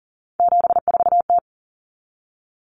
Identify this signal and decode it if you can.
Morse code, heard as a keyed tone, about 40 wpm, 700 Hz. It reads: 74T
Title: Common Cut Numbers - 3-Digit 40wpm